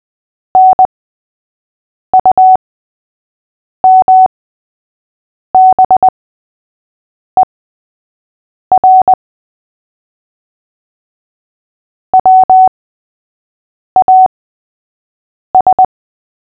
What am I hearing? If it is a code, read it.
Morse code, heard as a keyed tone, about 20 wpm, spaced as for 6 wpm, 750 Hz. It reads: NUMBER WAS
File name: cw / fists / Short-Story-3-6wpm